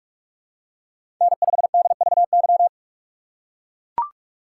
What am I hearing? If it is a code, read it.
Morse code, heard as a keyed tone, about 42 wpm, 700 Hz. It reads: N5DVY
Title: USA Call Signs - 5 Characters - 6x Character Spacing - 45wpm